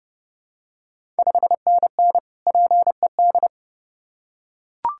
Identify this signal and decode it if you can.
Morse code, heard as a keyed tone, about 30 wpm, 700 Hz. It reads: 5NN PED
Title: Ontario QSO Party Exchange 30wpm